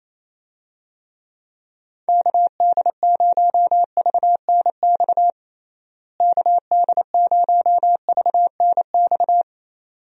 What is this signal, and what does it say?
Morse code, heard as a keyed tone, about 28 wpm, 700 Hz. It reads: KD0VNX KD0VNX
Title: USA Call Signs - 6 Characters - Repeated 2x - 28wpm